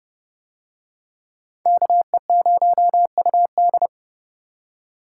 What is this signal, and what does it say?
Morse code, heard as a keyed tone, about 31 wpm, 700 Hz. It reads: KE0UD